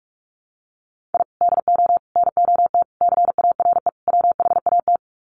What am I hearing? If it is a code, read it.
Morse code, heard as a keyed tone, about 44 wpm, 700 Hz. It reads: I DO NOT CARE WHAT
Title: Sets of 5 Words 45wpm